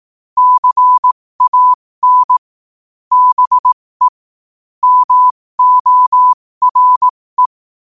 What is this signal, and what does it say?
Morse code, heard as a keyed tone, about 18 wpm, spaced as for 15 wpm, 1000 Hz. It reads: CAN BE MORE
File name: qotdfiles / 15wpm